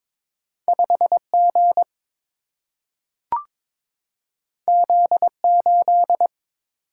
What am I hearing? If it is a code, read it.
Morse code, heard as a keyed tone, about 22 wpm, 700 Hz. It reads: 5 G Z 8
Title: Single-Letter Single-Number 22wpm